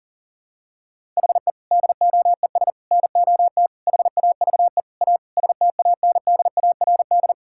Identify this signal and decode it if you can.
Morse code, heard as a keyed tone, about 40 wpm, 700 Hz. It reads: HE DOES NOT HAVE A STANDARD
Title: Sentences from Top 1000 Words - Review Part 6 - 40wpm